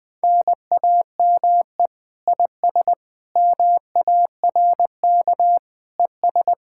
Morse code, heard as a keyed tone, about 20 wpm, 700 Hz. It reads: NAME IS MARK ES
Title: Speed-Racing Common QSO Phrases Encore - 17wpm